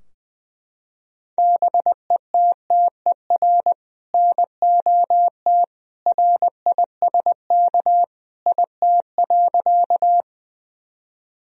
Morse code, heard as a keyed tone, about 20 wpm, 700 Hz. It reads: BETTER NOT RISK IT.
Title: Sentences from Top 1000 Words - Review Part 4 - 20wpm